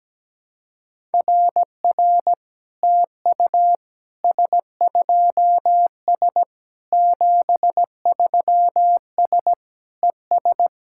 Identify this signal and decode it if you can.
Morse code, heard as a keyed tone, about 17 wpm, 700 Hz. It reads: RR TU S2S 73S ES
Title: SOTA QSO Practice 17wpm